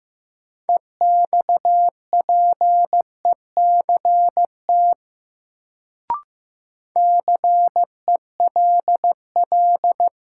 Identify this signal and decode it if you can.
Morse code, heard as a keyed tone, about 15 wpm, 700 Hz. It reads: EXPECT CELL
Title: Sentences from Top 1100 Words - New Word Drill Alt - 15wpm